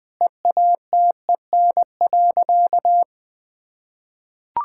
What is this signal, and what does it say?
Morse code, heard as a keyed tone, about 20 wpm, 700 Hz. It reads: EATEN.
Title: Sentences with 2-Word Phrases - Spoken Beginning - 20wpm